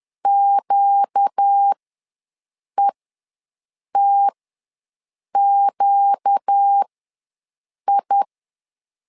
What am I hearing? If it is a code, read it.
Morse code, heard as a keyed tone, about 10 wpm, spaced as for 6 wpm, 800 Hz. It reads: QETQI
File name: LEZIONI / 4 Q ws 10wpm 2306